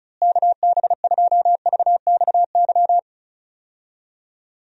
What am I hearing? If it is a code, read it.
Morse code, heard as a keyed tone, about 34 wpm, 700 Hz. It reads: KB2VXY
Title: USA Call Signs - 6 Characters - Repeated 3x - 35wpm